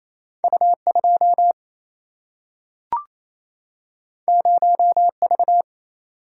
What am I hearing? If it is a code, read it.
Morse code, heard as a keyed tone, about 28 wpm, 700 Hz. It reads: U2 0V
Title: Single-Letter Single-Number 28wpm